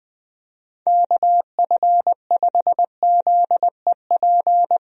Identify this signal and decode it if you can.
Morse code, heard as a keyed tone, about 20 wpm, 700 Hz. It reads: KF5ZEP
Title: USA Call Signs - 6 Characters - 20wpm